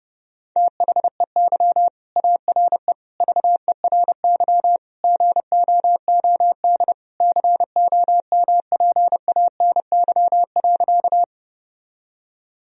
Morse code, heard as a keyed tone, about 32 wpm, 700 Hz. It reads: THEY ARE VERY GOOD COMPANY.
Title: Sentences from Top 1000 Words - Review Part 3 - 30wpm